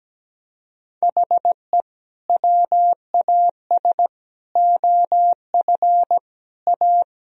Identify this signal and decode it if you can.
Morse code, heard as a keyed tone, about 17 wpm, 700 Hz. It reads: HE WAS OF A
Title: Sentences from Top 1000 Words - Review Part 1 - 17wpm